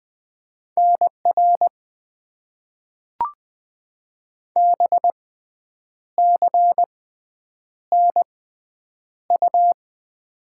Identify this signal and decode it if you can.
Morse code, heard as a keyed tone, about 20 wpm, spaced as for 8 wpm, 700 Hz. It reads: NR BCNU
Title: Top 90 QSO Elements - 6x Character Spacing - 20wpm